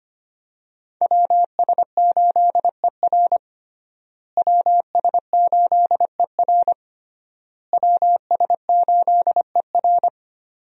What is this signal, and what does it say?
Morse code, heard as a keyed tone, about 25 wpm, 700 Hz. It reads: WS8ER WS8ER WS8ER
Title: USA Call Signs - 5 Characters - Repeated 3x - 25wpm